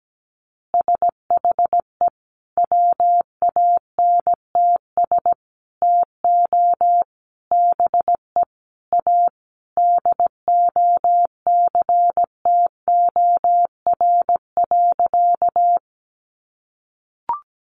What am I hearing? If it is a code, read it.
Morse code, heard as a keyed tone, about 17 wpm, 700 Hz. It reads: SHE WANTS TO BE A DOCTOR.
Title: Easy Phrases 17wpm